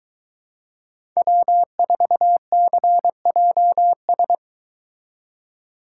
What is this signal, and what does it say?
Morse code, heard as a keyed tone, about 22 wpm, 700 Hz. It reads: W4CJS